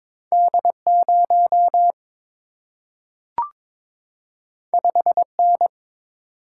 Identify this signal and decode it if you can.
Morse code, heard as a keyed tone, about 22 wpm, 700 Hz. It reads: D 0 5 N
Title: Single-Letter Single-Number 22wpm